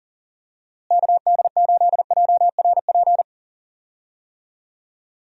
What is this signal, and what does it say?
Morse code, heard as a keyed tone, about 37 wpm, 700 Hz. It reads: KD8JRP
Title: USA Call Signs - 6 Characters - Repeated 3x - 40wpm